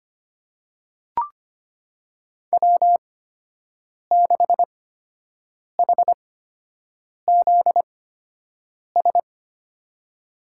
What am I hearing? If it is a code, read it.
Morse code, heard as a keyed tone, about 25 wpm, spaced as for 7 wpm, 700 Hz. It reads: W6HZS
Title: USA Call Signs - 8x Character Spacing - 25wpm